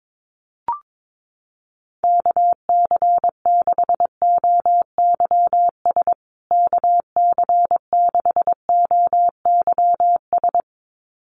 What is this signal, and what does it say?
Morse code, heard as a keyed tone, about 22 wpm, 700 Hz. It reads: KC6OYS KC6OYS
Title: USA Call Signs Repeated 2x 22wpm